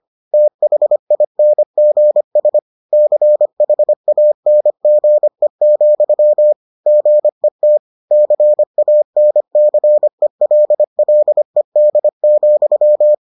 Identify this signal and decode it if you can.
Morse code, heard as a keyed tone, about 25 wpm, 600 Hz. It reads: THINGS CHANGE, GET CANCELLED,